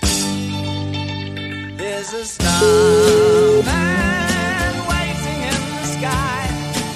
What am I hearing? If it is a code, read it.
Ringback tone (Europe, 425 Hz), with noise.